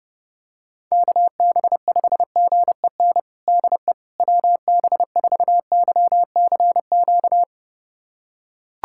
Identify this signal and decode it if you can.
Morse code, heard as a keyed tone, about 30 wpm, 700 Hz. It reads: KB5GEN DE WB4YCQ
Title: USA Callsign DE Callsign - 6 Characters - 30wpm